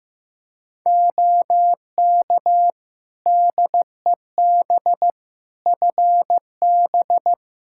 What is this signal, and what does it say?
Morse code, heard as a keyed tone, about 16 wpm, 700 Hz. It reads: OK DEB FB
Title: Practice QSO - Exchange 2 - Info - 15wpm